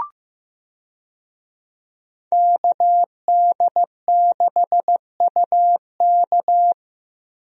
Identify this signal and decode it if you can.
Morse code, heard as a keyed tone, about 15 wpm, 700 Hz. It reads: KD6UK
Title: USA Call Signs - 5 Characters - Repeated 2x - 15wpm